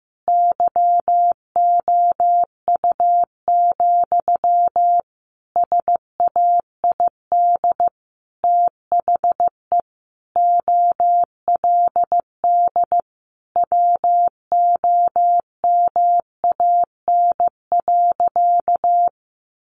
Morse code, heard as a keyed tone, about 15 wpm, 700 Hz. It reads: YOU, SAID THE OLD WOMAN.
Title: Sentences from Top 1100 Words 15wpm